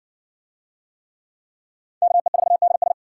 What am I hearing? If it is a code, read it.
Morse code, heard as a keyed tone, about 66 wpm, 700 Hz. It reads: KE4DS